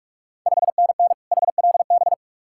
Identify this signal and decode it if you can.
Morse code, heard as a keyed tone, about 42 wpm, 700 Hz. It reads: 5NN HLB
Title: Ontario QSO Party Exchange 45wpm